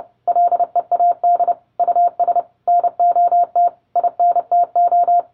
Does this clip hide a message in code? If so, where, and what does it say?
Yes: Morse code from 0.3 to 5.2 s, heard as a keyed tone, about 30 wpm, 700 Hz. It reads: LEAD US NOT INTO